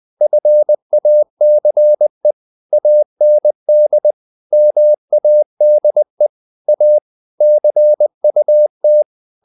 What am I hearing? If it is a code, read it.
Morse code, heard as a keyed tone, about 20 wpm, 600 Hz. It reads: FACE AND MADE A CUT
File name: the old man and the sea / the old man and the sea part 1 20wpm